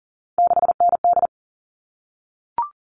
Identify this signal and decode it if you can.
Morse code, heard as a keyed tone, about 40 wpm, 700 Hz. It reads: 6ND
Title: All Cut Numbers - 3-Digit 40wpm